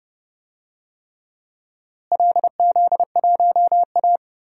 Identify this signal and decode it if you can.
Morse code, heard as a keyed tone, about 30 wpm, 700 Hz. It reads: LZ1A
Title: DXCC-100 Call Signs - 4 Characters - Repeated 3x - 30wpm